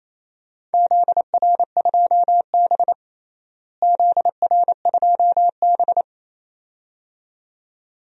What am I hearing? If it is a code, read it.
Morse code, heard as a keyed tone, about 28 wpm, 700 Hz. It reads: ZR2B ZR2B
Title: DXCC-100 Call Signs - 4 Characters - Repeated 2x - 28wpm